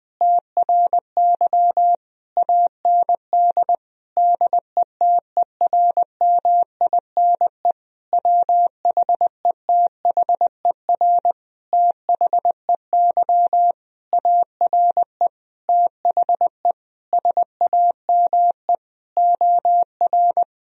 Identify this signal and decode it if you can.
Morse code, heard as a keyed tone, about 20 wpm, 700 Hz. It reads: TRY AND DETERMINE WHETHER THEY ARE THE SAME OR
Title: Sentences from Top 1000 Words - Review Part 4 - 20wpm